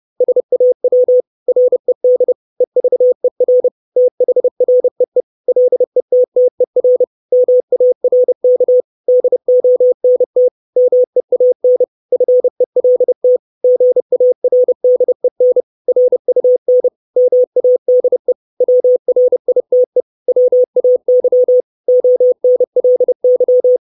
Morse code, heard as a keyed tone, about 30 wpm, 500 Hz. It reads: SAW RED EVER THREE LETTER MARK DONT MEAN FELT GARDEN RUN MADE WRITE WAY ONLY
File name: 632 Words 30WPM